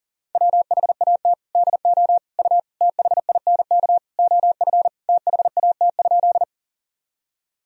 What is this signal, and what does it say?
Morse code, heard as a keyed tone, about 40 wpm, 700 Hz. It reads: WHAT DO U THINK OF THAT?
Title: Sentences from Top 200 Words and Abbreviations 40wpm